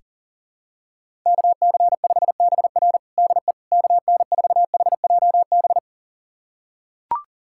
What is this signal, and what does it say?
Morse code, heard as a keyed tone, about 40 wpm, 700 Hz. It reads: KC5BR DE KN4HJB
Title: USA Callsign DE Callsign 40wpm